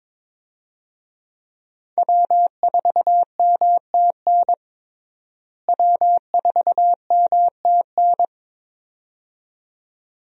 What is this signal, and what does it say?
Morse code, heard as a keyed tone, about 22 wpm, 700 Hz. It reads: W4MTN W4MTN
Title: USA Call Signs - 5 Characters - Repeated 2x - 22wpm